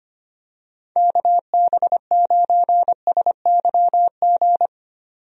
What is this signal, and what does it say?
Morse code, heard as a keyed tone, about 25 wpm, 700 Hz. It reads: KB9SYG
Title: USA Call Signs - 6 Characters - 25wpm